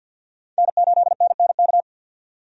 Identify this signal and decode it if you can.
Morse code, heard as a keyed tone, about 52 wpm, 700 Hz. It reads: N8NNK
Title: USA Call Signs - 5 Characters - 50wpm